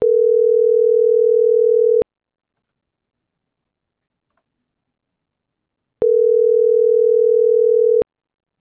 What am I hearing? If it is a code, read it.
Ringback tone (North America), 440 + 480 Hz.
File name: CENTRAL TERRESTRE / out karis - .77486